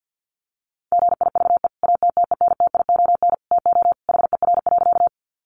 Morse code, heard as a keyed tone, about 50 wpm, 700 Hz. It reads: GIVE ATTENTION TO HER.
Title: Sentences from Top 1000 Words 50wpm